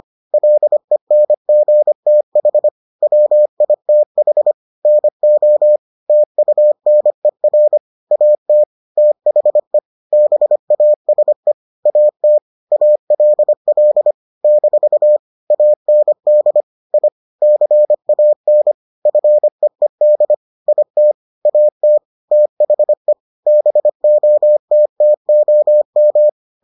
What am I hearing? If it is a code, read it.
Morse code, heard as a keyed tone, about 25 wpm, 600 Hz. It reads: LENGTH WITH NO TUNER AT THE BASE AT ALL - AND I CAN FEED IT AT THE BOTTOM